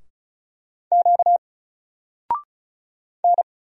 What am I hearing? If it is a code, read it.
Morse code, heard as a keyed tone, about 35 wpm, 700 Hz. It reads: Q N
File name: Single-Letter-Number-35wpm